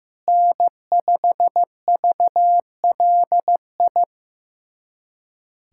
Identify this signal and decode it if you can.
Morse code, heard as a keyed tone, about 15 wpm, 700 Hz. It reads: N5VLI
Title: USA Callsign DE Callsign - 5 Characters - 15wpm